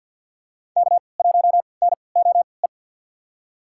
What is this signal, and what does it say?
Morse code, heard as a keyed tone, about 52 wpm, spaced as for 29 wpm, 700 Hz. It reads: K1NOE
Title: USA Call Signs - 5 Characters - 3x Character Spacing - 50wpm